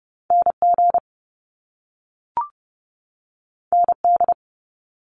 Morse code, heard as a keyed tone, about 30 wpm, 700 Hz. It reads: NG ND
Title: Top Two Letter Combinations in Words 30wpm